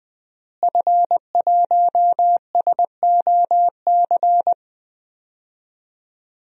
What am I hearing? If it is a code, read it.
Morse code, heard as a keyed tone, about 20 wpm, 700 Hz. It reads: F1SOC
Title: DXCC-100 Call Signs - 5 Characters - Repeated 2x - 20wpm